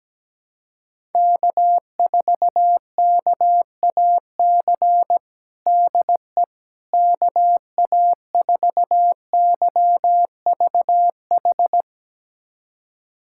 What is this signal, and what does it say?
Morse code, heard as a keyed tone, about 17 wpm, 700 Hz. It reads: K4KAC DE KA4YVH
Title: USA Callsign DE Callsign 17wpm